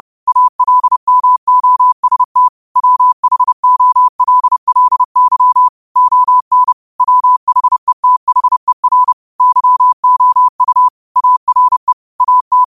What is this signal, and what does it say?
Morse code, heard as a keyed tone, about 30 wpm, 1000 Hz. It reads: ALMOST WHOLLY ON WHETHER YOU ARE AT
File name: qotdfiles / 30wpm